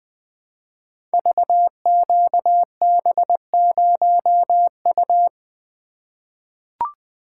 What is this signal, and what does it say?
Morse code, heard as a keyed tone, about 20 wpm, 700 Hz. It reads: VQB0U
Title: Code Groups - 5 Letter:Numbers - 20wpm